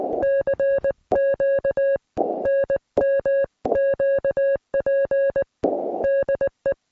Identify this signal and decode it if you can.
Morse code, heard as a keyed tone, about 19 wpm, 1650 Hz. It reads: CQ NMQP DE